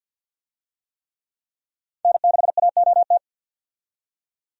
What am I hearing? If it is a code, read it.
Morse code, heard as a keyed tone, about 50 wpm, 700 Hz. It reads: N6AOT
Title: USA Call Signs - 5 Characters - 50wpm